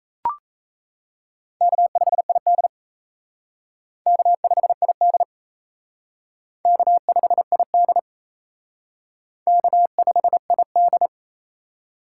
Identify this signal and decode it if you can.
Morse code, heard as a keyed tone, about 35 wpm, 700 Hz. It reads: K5ID K5ID K5ID K5ID
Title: Speed Racing - USA Call Signs 28wpm